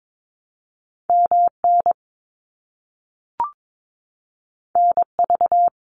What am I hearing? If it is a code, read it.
Morse code, heard as a keyed tone, about 22 wpm, 700 Hz. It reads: MN NV
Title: US States Repeated 3x 22wpm